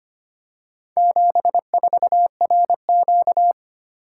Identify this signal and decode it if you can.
Morse code, heard as a keyed tone, about 25 wpm, 700 Hz. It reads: 74RQ
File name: practice / 25wpm / Four Letter-Number Combinations - 25wpm